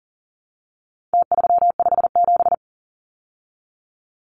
Morse code, heard as a keyed tone, about 40 wpm, 700 Hz. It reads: T357
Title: Numbers - Contest Style - 4-Digit 40wpm